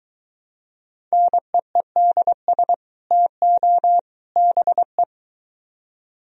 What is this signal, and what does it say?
Morse code, heard as a keyed tone, about 23 wpm, 700 Hz. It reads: NEEDS TO BE